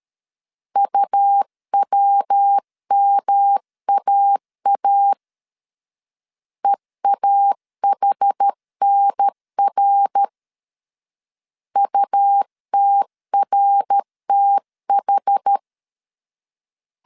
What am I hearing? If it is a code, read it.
Morse code, heard as a keyed tone, about 13 wpm, 800 Hz. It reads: UWMAA EAHNR UTRTH
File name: CW / DET / 1 3 1500C S A 12wpm 2905